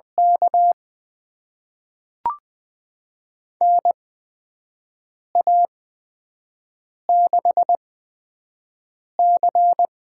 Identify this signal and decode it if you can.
Morse code, heard as a keyed tone, about 20 wpm, spaced as for 5 wpm, 700 Hz. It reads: K NA6C
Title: USA Call Signs - 8x Character Spacing - 20wpm